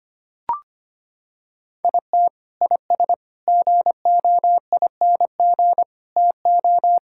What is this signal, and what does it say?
Morse code, heard as a keyed tone, about 25 wpm, 700 Hz. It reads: IT IS GOING TO
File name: practice / 25wpm / Sentences with 3-Word Phrases - 25wpm